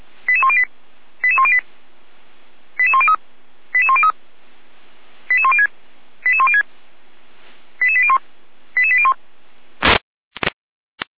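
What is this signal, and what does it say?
Five-tone selective call (ZVEI), heard as quick runs of five tones, tones of about 70 ms. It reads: 88108 88108 88103 88103 88106 88106 88801 88801
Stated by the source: ZVEI five-tone call received by SDR.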